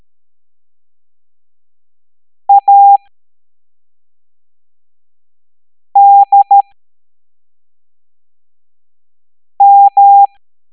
Morse code, heard as a keyed tone, about 13 wpm, 800 Hz. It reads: A D M
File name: hamdata / L07-EX03-HQP